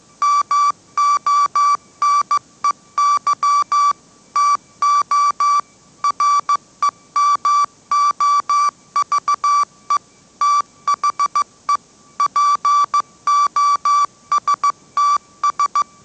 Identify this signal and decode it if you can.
Morse code, heard as a keyed tone, about 15 wpm, 1160 Hz. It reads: MONEY TO REMOVE THE POSTS